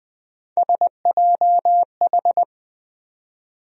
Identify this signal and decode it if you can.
Morse code, heard as a keyed tone, about 20 wpm, 700 Hz. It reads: SJH
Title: USA Call Sign Postfixes - 3 Letters 20wpm